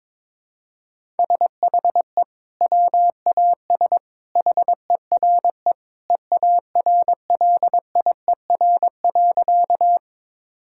Morse code, heard as a keyed tone, about 22 wpm, 700 Hz. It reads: SHE WAS HERE EARLIER.
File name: practice / 22wpm / Sentences with 2-Word Phrases - 22wpm